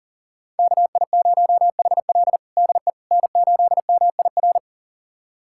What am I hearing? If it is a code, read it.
Morse code, heard as a keyed tone, about 40 wpm, 700 Hz. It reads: KI0HL DE N8MIR
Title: USA Callsign DE Callsign - 5 Characters - 40wpm